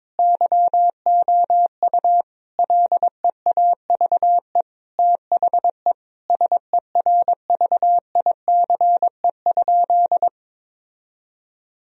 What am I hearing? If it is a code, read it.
Morse code, heard as a keyed tone, about 22 wpm, 700 Hz. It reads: YOU LEAVE THE SERVICE?
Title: Sentences from Top 1000 Words - Review Part 4 - 22wpm